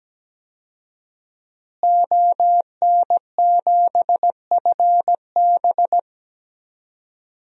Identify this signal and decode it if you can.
Morse code, heard as a keyed tone, about 17 wpm, 700 Hz. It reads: ON7FB
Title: DXCC-100 Call Signs - 5 Characters - Repeated 3x - 17wpm